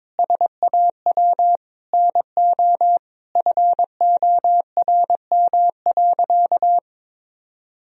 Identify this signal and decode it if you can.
Morse code, heard as a keyed tone, about 21 wpm, 700 Hz. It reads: SAW NO FORM.
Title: Sentences from Top 1000 Words - Review Part 1 - 22wpm